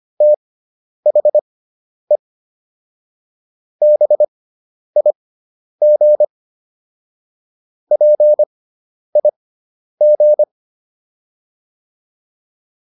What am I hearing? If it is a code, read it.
Morse code, heard as a keyed tone, about 25 wpm, spaced as for 10 wpm, 600 Hz. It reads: THE BIG PIG